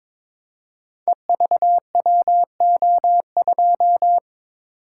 Morse code, heard as a keyed tone, about 22 wpm, 700 Hz. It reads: EVWO2
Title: Code Groups - 5 Letter:Numbers - 22wpm